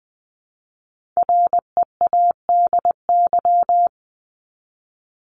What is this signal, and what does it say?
Morse code, heard as a keyed tone, about 20 wpm, 700 Hz. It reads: READY
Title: Top 500 Words 20wpm